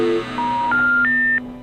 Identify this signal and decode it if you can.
Special information tone, with noise.